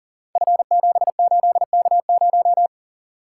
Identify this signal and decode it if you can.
Morse code, heard as a keyed tone, about 40 wpm, 690 Hz. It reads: F78K0
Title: Code Groups - 5 Letter:Numbers - 40wpm